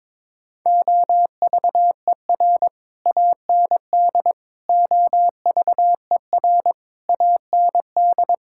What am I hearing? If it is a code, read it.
Morse code, heard as a keyed tone, about 22 wpm, 700 Hz. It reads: OVER AND OVER AND
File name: practice / 22wpm / Sets of 6 Words - 22wpm